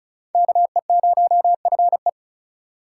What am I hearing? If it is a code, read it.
Morse code, heard as a keyed tone, about 42 wpm, 700 Hz. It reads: KE0FE
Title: Speed Racing - USA 2x2 Call Signs 35wpm